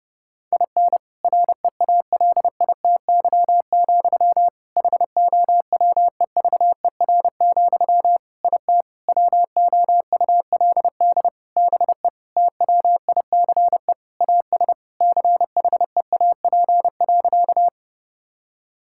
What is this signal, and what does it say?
Morse code, heard as a keyed tone, about 30 wpm, 700 Hz. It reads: IN REALITY, HOWEVER, IT WOULD BE TWICE AS CHEAP.
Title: Sentences from Top 1600 Words 30wpm